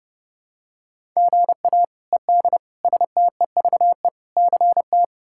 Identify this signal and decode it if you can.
Morse code, heard as a keyed tone, about 30 wpm, 700 Hz. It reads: GA ED STEVE CT
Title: K1USN SST QSO Practice - 30wpm